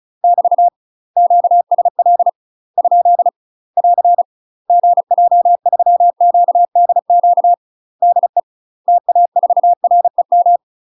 Morse code, heard as a keyed tone, about 35 wpm, 700 Hz. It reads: <BT> QSL ? <AR> GJ3QDQ DE TA4REM
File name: UK MASTER QSOs TRAINING / 35WPM / UK MASTER QSOs 0029